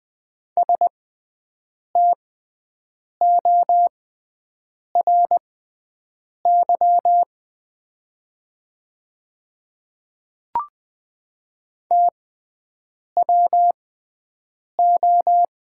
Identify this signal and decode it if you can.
Morse code, heard as a keyed tone, about 20 wpm, spaced as for 7 wpm, 700 Hz. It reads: STORY TWO
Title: 6x Character Spacing - Top 300 Words - 20wpm